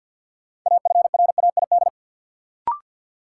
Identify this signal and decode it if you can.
Morse code, heard as a keyed tone, about 52 wpm, 700 Hz. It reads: AFRAID
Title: Sentences from Top 1500 Words - New Word Drill Alt - 50wpm